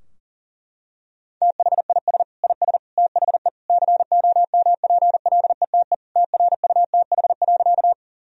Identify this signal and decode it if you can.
Morse code, heard as a keyed tone, about 40 wpm, 700 Hz. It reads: THIS IS THE COMPLETE TRUTH.
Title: Sentences from Top 1000 Words - Review Part 2 - 40wpm